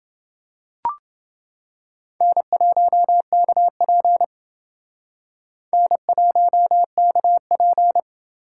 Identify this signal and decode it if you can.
Morse code, heard as a keyed tone, about 28 wpm, 700 Hz. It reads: N1KP N1KP